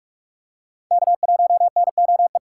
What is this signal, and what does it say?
Morse code, heard as a keyed tone, about 46 wpm, 700 Hz. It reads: K1NOE
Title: USA Call Signs - 5 Characters - 2x Character Spacing - 45wpm